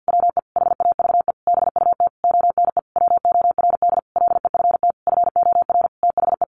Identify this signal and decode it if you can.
Morse code, heard as a keyed tone, about 50 wpm, 700 Hz. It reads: WE HAVE BUT ONE WORD LEFT FOR THE
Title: Sentences from Top 1500 Words 50wpm